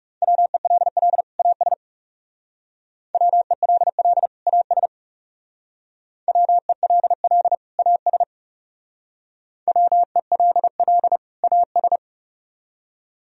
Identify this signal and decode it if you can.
Morse code, heard as a keyed tone, about 35 wpm, 700 Hz. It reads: WELL AS WELL AS WELL AS WELL AS
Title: Speed-Racing - Sets of 2 Words 30wpm